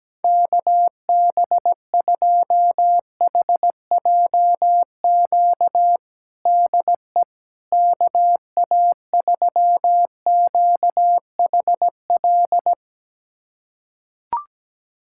Morse code, heard as a keyed tone, about 17 wpm, 700 Hz. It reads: KB2HJQ DE KA3QHL
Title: USA Callsign DE Callsign 17wpm